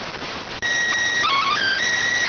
SSTV (slow-scan television), PD 290.